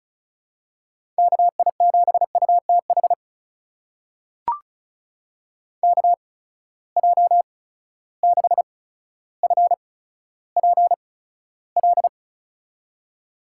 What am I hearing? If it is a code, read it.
Morse code, heard as a keyed tone, about 36 wpm, spaced as for 15 wpm, 700 Hz. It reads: KI7UTH KJ6FPL
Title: USA Call Signs - 6 Characters - 8x Character Spacing - 35wpm